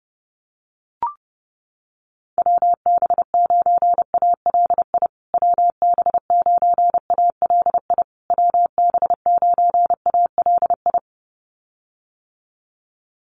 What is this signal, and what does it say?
Morse code, heard as a keyed tone, about 30 wpm, 700 Hz. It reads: WB9ALI WB9ALI WB9ALI
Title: USA Call Signs Repeated 3x 30wpm